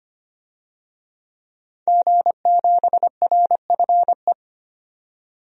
Morse code, heard as a keyed tone, about 25 wpm, 700 Hz. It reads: G7RFE
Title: DXCC-100 Call Signs - 5 Characters - 25wpm